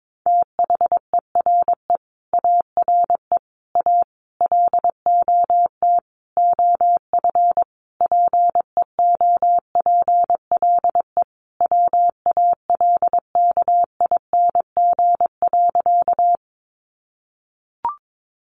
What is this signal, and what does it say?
Morse code, heard as a keyed tone, about 22 wpm, 700 Hz. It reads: THERE ARE A LOT OF PEOPLE WALKING.
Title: Easy Phrases 22wpm